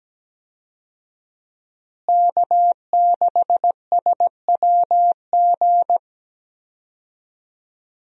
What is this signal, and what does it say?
Morse code, heard as a keyed tone, about 17 wpm, 700 Hz. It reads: K6SWG
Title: USA Call Signs - 5 Characters - Repeated 3x - 17wpm